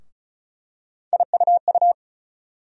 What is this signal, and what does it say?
Morse code, heard as a keyed tone, about 34 wpm, 700 Hz. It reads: IUU
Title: USA Call Sign Postfixes - 3 Letters 35wpm